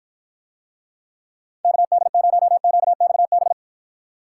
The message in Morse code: KD0QXB

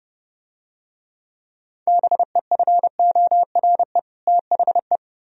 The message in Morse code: BEFORE THE

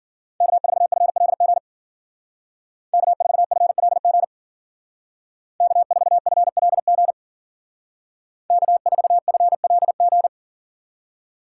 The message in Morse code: K4FLG K4FLG K4FLG K4FLG

